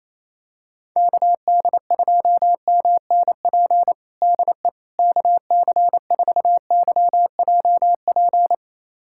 Morse code, heard as a keyed tone, about 27 wpm, 700 Hz. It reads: KD2MNP DE KC4YJP
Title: USA Callsign DE Callsign 28wpm